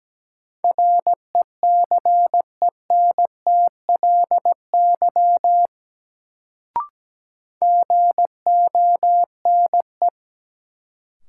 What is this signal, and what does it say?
Morse code, heard as a keyed tone, about 17 wpm, 700 Hz. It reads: RECENTLY GONE
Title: Sentences from Top 1200 Words - New Word Drill Alt - 17wpm